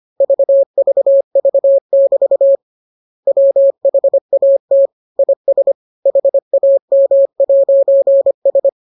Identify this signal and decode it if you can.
Morse code, heard as a keyed tone, about 25 wpm, 550 Hz. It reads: VVV<BT> WHAT IS HAM'S